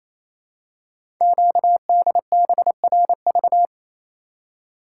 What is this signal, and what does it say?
Morse code, heard as a keyed tone, about 29 wpm, 700 Hz. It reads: QDBRV
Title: Code Groups - 5 Letters - 28wpm